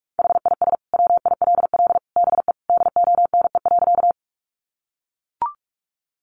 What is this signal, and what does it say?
Morse code, heard as a keyed tone, about 44 wpm, 700 Hz. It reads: HIS WILL BE DONE.
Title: Sentences from Top 500 Words 45wpm